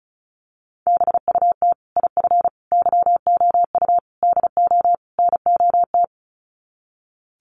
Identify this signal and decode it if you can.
Morse code, heard as a keyed tone, about 34 wpm, 700 Hz. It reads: BUT IF YOU DO NOT
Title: Sets of 5 Words Encore 35wpm